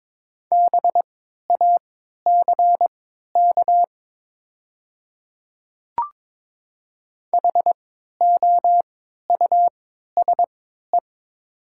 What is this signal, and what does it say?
Morse code, heard as a keyed tone, about 22 wpm, spaced as for 13 wpm, 700 Hz. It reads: BACK HOUSE
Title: 3x Character Spacing - Top 200 Words - 22wpm